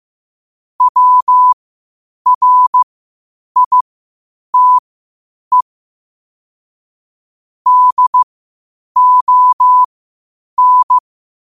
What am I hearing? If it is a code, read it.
Morse code, heard as a keyed tone, about 15 wpm, spaced as for 8 wpm, 1000 Hz. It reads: WRITE DON
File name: qotdfiles / 8wpm